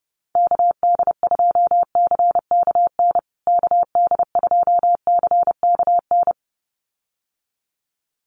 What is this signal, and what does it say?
Morse code, heard as a keyed tone, about 28 wpm, 700 Hz. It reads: KD2CKN KD2CKN
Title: USA Call Signs Repeated 2x 30wpm